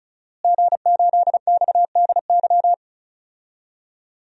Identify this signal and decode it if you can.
Morse code, heard as a keyed tone, about 34 wpm, 700 Hz. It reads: G8XDY
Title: DXCC-100 Call Signs - 5 Characters - Repeated 3x - 35wpm